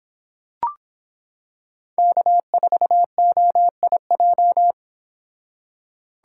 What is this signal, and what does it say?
Morse code, heard as a keyed tone, about 24 wpm, 700 Hz. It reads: K4OIJ